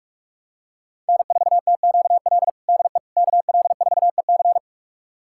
Morse code, heard as a keyed tone, about 46 wpm, 700 Hz. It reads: N4TQL DE KL4EC
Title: USA Callsign DE Callsign - 5 Characters - 45wpm